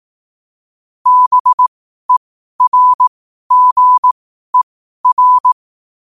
Morse code, heard as a keyed tone, about 18 wpm, spaced as for 12 wpm, 1000 Hz. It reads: BERGER
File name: qotdfiles / 12wpm